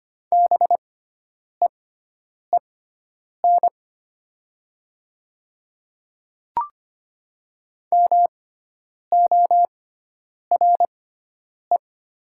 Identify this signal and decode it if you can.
Morse code, heard as a keyed tone, about 25 wpm, spaced as for 9 wpm, 700 Hz. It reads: BEEN MORE
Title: 6x Character Spacing - Top 100 Words - 25wpm